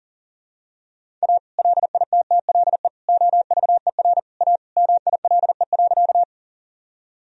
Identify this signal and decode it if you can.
Morse code, heard as a keyed tone, about 37 wpm, 700 Hz. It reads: A LITTLE OVER A MILE.
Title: Sentences from Top 1000 Words - Review Part 1 - 40wpm